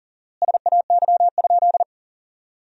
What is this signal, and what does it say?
Morse code, heard as a keyed tone, about 40 wpm, 700 Hz. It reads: SAY?
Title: 4x Word-Spacing - Sentences from Top 200 Words - 40wpm